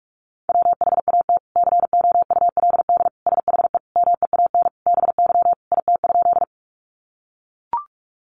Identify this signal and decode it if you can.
Morse code, heard as a keyed tone, about 44 wpm, 700 Hz. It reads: WHAT COULD SHE MEAN BY IT?
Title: Sentences from Top 200 Words 45wpm